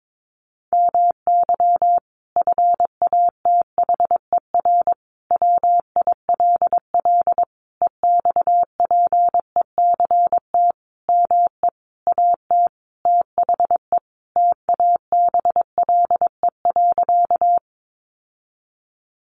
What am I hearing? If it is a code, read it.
Morse code, heard as a keyed tone, about 22 wpm, 700 Hz. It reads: MY FATHER WILL EXPECT ME AT THE TABLE.
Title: Sentences from Top 1100 Words 22wpm